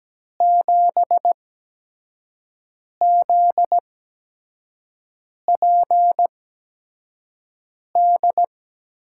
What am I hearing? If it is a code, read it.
Morse code, heard as a keyed tone, about 17 wpm, spaced as for 5 wpm, 700 Hz. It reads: 7ZPD
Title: USA Call Signs - 6 Characters - 8x Character Spacing - 17wpm